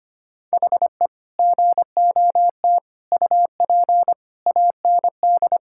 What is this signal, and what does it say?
Morse code, heard as a keyed tone, about 25 wpm, 700 Hz. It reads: HE GOT UP AND
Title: Sentences from Top 1600 Words 25wpm